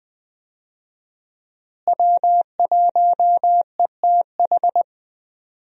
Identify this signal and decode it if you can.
Morse code, heard as a keyed tone, about 21 wpm, 700 Hz. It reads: W1ETH
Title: USA Call Signs - 5 Characters - Repeated 2x - 20wpm